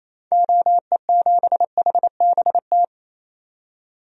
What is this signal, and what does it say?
Morse code, heard as a keyed tone, about 28 wpm, 700 Hz. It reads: OE7HBT
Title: DXCC-100 Call Signs - 6 Characters - 28wpm